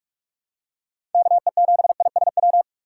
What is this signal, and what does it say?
Morse code, heard as a keyed tone, about 48 wpm, 700 Hz. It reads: KE7ISW
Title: USA Call Signs - 6 Characters - 6x Character Spacing - 45wpm